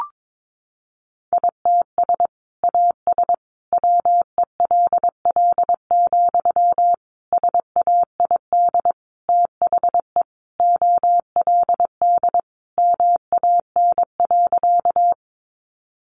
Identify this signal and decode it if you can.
Morse code, heard as a keyed tone, about 22 wpm, 700 Hz. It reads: ITS AS WELL, SAID THE OLD MAN.
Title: Sentences from Top 600 Words 22wpm